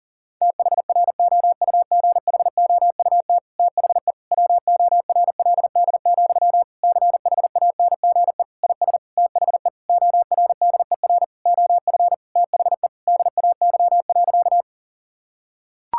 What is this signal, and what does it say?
Morse code, heard as a keyed tone, about 40 wpm, 700 Hz. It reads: THROUGHOUT THE WORLD, CHANGE IS THE ORDER OF THE DAY.
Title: Sentences from Top 1000 Words - Review Part 5 - 40wpm